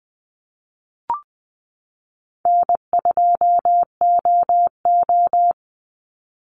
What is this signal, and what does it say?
Morse code, heard as a keyed tone, about 20 wpm, 700 Hz. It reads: N2OO
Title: Easy Call Signs Repeated 3x - 20wpm